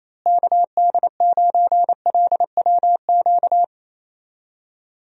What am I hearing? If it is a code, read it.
Morse code, heard as a keyed tone, about 28 wpm, 700 Hz. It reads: KD9LWQ
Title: USA Call Signs - 6 Characters - Repeated 3x - 28wpm